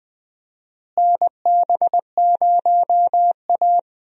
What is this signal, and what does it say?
Morse code, heard as a keyed tone, about 21 wpm, 690 Hz. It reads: NB0A
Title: USA Call Signs - 4 Characters - Repeated 3x - 20wpm